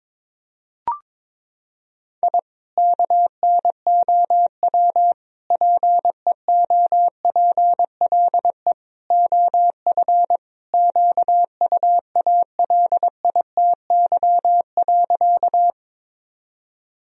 Morse code, heard as a keyed tone, about 22 wpm, 700 Hz. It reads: I KNOW PEOPLE OF QUALITY.
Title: Sentences from Top 1000 Words - Review Part 4 - 22wpm